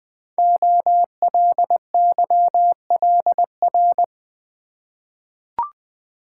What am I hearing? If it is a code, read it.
Morse code, heard as a keyed tone, about 21 wpm, 700 Hz. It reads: OLYLR
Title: Code Groups - 5 Letter:Numbers - 20wpm